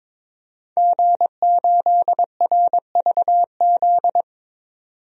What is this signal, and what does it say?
Morse code, heard as a keyed tone, about 22 wpm, 700 Hz. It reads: G8RVZ
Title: DXCC-100 Call Signs - 5 Characters - 22wpm